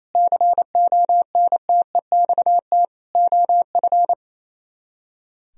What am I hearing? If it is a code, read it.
Morse code, heard as a keyed tone, about 32 wpm, 700 Hz. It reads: CONTEXT OF
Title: Speed-Racing - Sets of 4 Words 28wpm